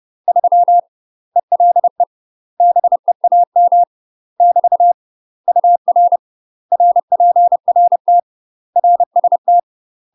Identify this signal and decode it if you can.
Morse code, heard as a keyed tone, about 30 wpm, 700 Hz. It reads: 3 ELE BEAM <BT> UR RPRT RST
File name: UK MASTER QSOs TRAINING / 30WPM / UK MASTER QSOs 0020